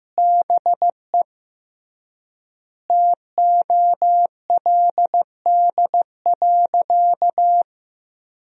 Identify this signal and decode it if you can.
Morse code, heard as a keyed tone, about 15 wpm, 700 Hz. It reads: BE TOLD.
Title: 3x Word-Spacing - Sentences from Top 400 Words - 15wpm